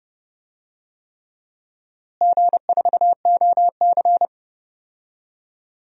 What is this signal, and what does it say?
Morse code, heard as a keyed tone, about 32 wpm, 700 Hz. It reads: G4OC
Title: DXCC-100 Call Signs - 4 Characters - Repeated 3x - 30wpm